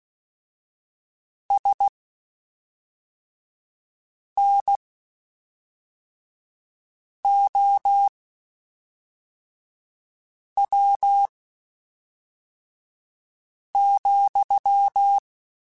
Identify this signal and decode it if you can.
Morse code, heard as a keyed tone, about 16 wpm, spaced as for 3 wpm, 800 Hz. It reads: SNOW,
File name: (spaced) / QSOs Spaced-mp3s / WA7QOX 1 @5 sp